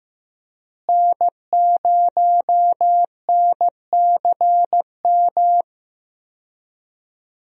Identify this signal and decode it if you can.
Morse code, heard as a keyed tone, about 16 wpm, 700 Hz. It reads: N0NCM
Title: USA Call Signs - 5 Characters - Repeated 3x - 15wpm